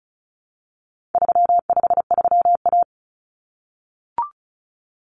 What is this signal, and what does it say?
Morse code, heard as a keyed tone, about 34 wpm, 700 Hz. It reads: 353A
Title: Common Cut Numbers - 4-Digit 35wpm